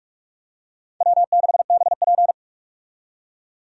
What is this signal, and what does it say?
Morse code, heard as a keyed tone, about 46 wpm, 700 Hz. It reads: W6BP
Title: USA Call Signs - 4 Characters - 45wpm